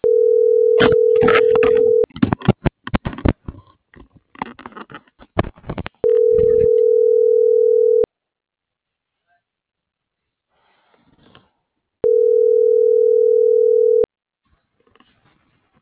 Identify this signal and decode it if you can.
Ringback tone (North America), 440 + 480 Hz.